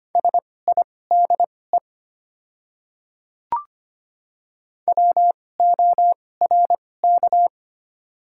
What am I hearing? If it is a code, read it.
Morse code, heard as a keyed tone, about 25 wpm, spaced as for 18 wpm, 700 Hz. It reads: SIDE WORK
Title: 2x Character Spacing - Top 300 Words - 25wpm